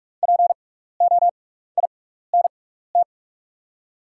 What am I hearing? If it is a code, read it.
Morse code, heard as a keyed tone, about 44 wpm, spaced as for 15 wpm, 700 Hz. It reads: POINT
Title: 6x Character Spacing - Top 300 Words - 45wpm